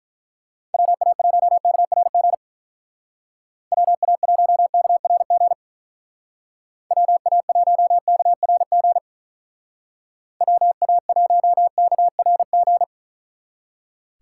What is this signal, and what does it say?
Morse code, heard as a keyed tone, about 44 wpm, 700 Hz. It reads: WA1KRG WA1KRG WA1KRG WA1KRG